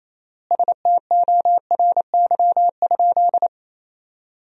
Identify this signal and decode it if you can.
Morse code, heard as a keyed tone, about 28 wpm, 700 Hz. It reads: STORY?
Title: 4x Word-Spacing - Sentences from Top 400 Words - 28wpm